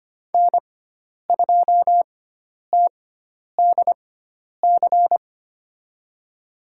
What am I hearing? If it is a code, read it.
Morse code, heard as a keyed tone, about 25 wpm, spaced as for 10 wpm, 700 Hz. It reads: N2TDC
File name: Lesson 16g - Letter c - Call Signs 25wpm